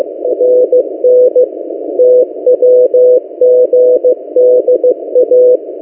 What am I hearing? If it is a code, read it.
Morse code, heard as a keyed tone, about 15 wpm, 500 Hz. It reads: NN TWGDA
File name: M08a- -1355utc-8096khz